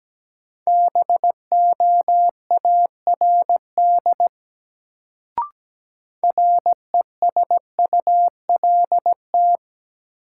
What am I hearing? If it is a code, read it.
Morse code, heard as a keyed tone, about 17 wpm, 700 Hz. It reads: BOARD RESULT
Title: Sentences from Top 700 Words - New Word Drill - 17wpm